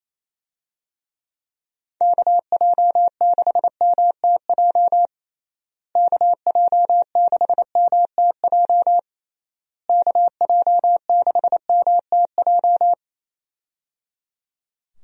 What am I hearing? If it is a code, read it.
Morse code, heard as a keyed tone, about 28 wpm, 700 Hz. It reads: KJ6MTJ KJ6MTJ KJ6MTJ
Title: USA Call Signs - 6 Characters - Repeated 3x - 28wpm